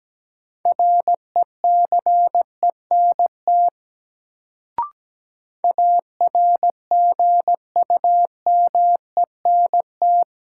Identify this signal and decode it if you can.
Morse code, heard as a keyed tone, about 17 wpm, 700 Hz. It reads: RECENT ARGUMENT
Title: Sentences from Top 1400 Words - New Word Drill Alt - 17wpm